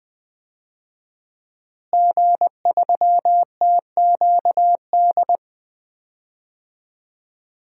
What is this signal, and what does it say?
Morse code, heard as a keyed tone, about 20 wpm, 700 Hz. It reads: G3TQD
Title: DXCC-100 Call Signs - 5 Characters - Repeated 3x - 20wpm